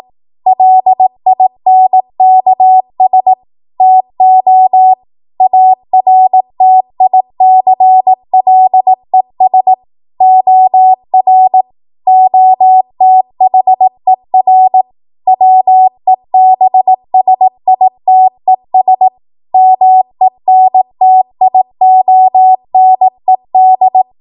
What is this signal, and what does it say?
Morse code, heard as a keyed tone, about 18 wpm, 750 Hz. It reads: LINKS TO ARTICLES OR OTHER WEBSITES MENTIONED
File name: Morse / 18WPM